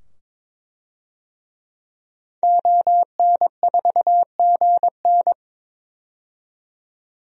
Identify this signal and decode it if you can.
Morse code, heard as a keyed tone, about 22 wpm, 700 Hz. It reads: ON4GN
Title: DXCC-100 Call Signs - 5 Characters - 22wpm